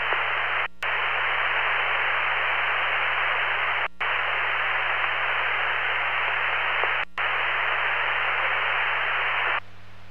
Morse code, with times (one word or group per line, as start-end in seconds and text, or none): none